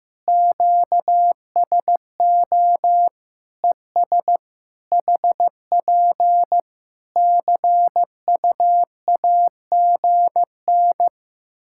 Morse code, heard as a keyed tone, about 15 wpm, 700 Hz. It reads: QSO ES HP CUAGN
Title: K1USN SST QSO Practice 15wpm